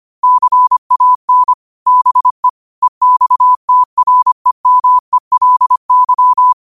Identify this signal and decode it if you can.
Morse code, heard as a keyed tone, about 25 wpm, 1000 Hz. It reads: CAN BE EXTREMELY
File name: qotdfiles / 25wpm